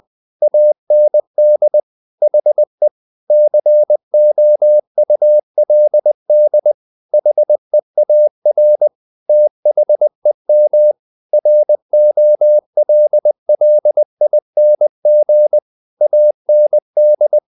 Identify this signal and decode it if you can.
Morse code, heard as a keyed tone, about 20 wpm, 600 Hz. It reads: AND HE COULD HEAR THEM ROLLING AND
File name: the old man and the sea / the old man and the sea part 1 20wpm